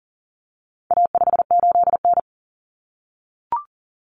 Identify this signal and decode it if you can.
Morse code, heard as a keyed tone, about 40 wpm, 700 Hz. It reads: A58N
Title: Common Cut Numbers - 4-Digit 40wpm